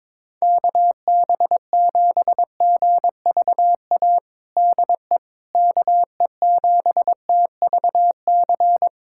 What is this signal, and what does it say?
Morse code, heard as a keyed tone, about 22 wpm, 700 Hz. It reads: KB7GVA DE KE7TVC
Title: USA Callsign DE Callsign - 6 Characters - 22wpm